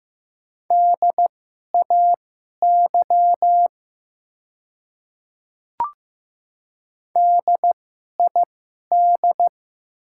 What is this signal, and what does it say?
Morse code, heard as a keyed tone, about 15 wpm, spaced as for 11 wpm, 700 Hz. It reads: DAY DID
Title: 2x Character Spacing - Top 100 Words - 15wpm